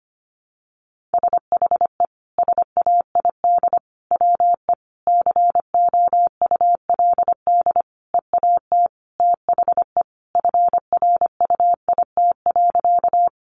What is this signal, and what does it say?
Morse code, heard as a keyed tone, about 25 wpm, 700 Hz. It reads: SHE SAID WE COULD EAT THE FRUIT.
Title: Sentences from Top 1300 Words 25wpm